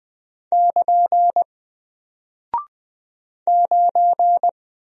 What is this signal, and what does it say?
Morse code, heard as a keyed tone, about 20 wpm, 700 Hz. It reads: <KN> 9
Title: Single Character (Letter:Number:Punctuation:Prosign) Alt 20wpm